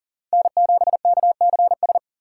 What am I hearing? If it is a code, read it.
Morse code, heard as a keyed tone, about 40 wpm, 700 Hz. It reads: N7KCS
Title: USA Call Signs - 4x Character Spacing - 40wpm